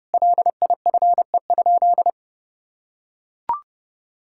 Morse code, heard as a keyed tone, about 30 wpm, 700 Hz. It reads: LIFE?
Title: Sentences from Top 1000 Words - Review Part 6 - 30wpm